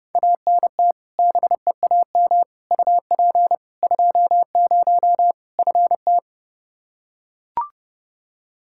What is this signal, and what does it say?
Morse code, heard as a keyed tone, about 30 wpm, 700 Hz. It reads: ANT BEAM UP 20 FT
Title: K1USN SST QSO Practice 30wpm